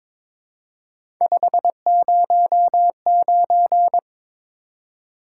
Morse code, heard as a keyed tone, about 22 wpm, 700 Hz. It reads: 509
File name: practice / 22wpm / Three Letter-Number Combinations - 22wpm